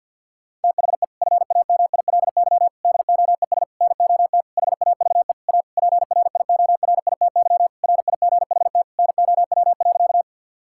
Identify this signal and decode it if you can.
Morse code, heard as a keyed tone, about 50 wpm, 700 Hz. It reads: THE FAMILY DOES NOT HAVE A PRIORITY RIGHT NOW.
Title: Sentences from Top 1600 Words 50wpm